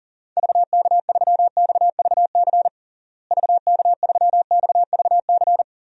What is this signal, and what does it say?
Morse code, heard as a keyed tone, about 40 wpm, 700 Hz. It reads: VK3XVC VK3XVC
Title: DXCC-100 Call Signs - 6 Characters - Repeated 3x - 40wpm